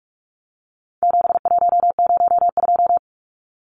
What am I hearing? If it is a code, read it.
Morse code, heard as a keyed tone, about 44 wpm, 700 Hz. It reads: 7102
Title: Numbers - 4-Digit 45wpm